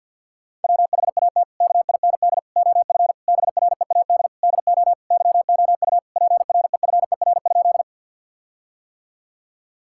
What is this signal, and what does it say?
Morse code, heard as a keyed tone, about 50 wpm, 700 Hz. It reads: WHAT KIND OF BREAD DO YOU PREFER?
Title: Sentences from Top 1600 Words 50wpm